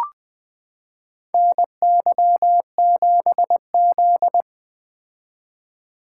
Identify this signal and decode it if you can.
Morse code, heard as a keyed tone, about 21 wpm, 700 Hz. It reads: NY7Z
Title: K1USN SST QSO Practice - 20wpm